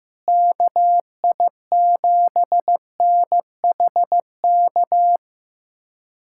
Morse code, heard as a keyed tone, about 15 wpm, 700 Hz. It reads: KI7NHK
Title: USA Callsign DE Callsign 15wpm